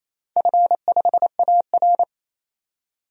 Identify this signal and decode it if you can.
Morse code, heard as a keyed tone, about 28 wpm, 700 Hz. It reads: F5AR
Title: DXCC-100 Call Signs - 4 Characters - 28wpm